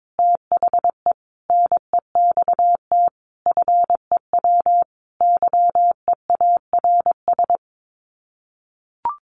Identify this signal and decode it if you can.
Morse code, heard as a keyed tone, about 22 wpm, 700 Hz. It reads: THE NEXT FEW YEARS
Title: Sets of 5 Words Encore 22wpm